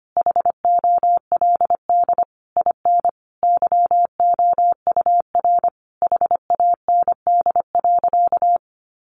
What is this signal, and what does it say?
Morse code, heard as a keyed tone, about 25 wpm, 700 Hz. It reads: HOLD IN YOUR HAND.
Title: Sentences from Top 400 Words 25wpm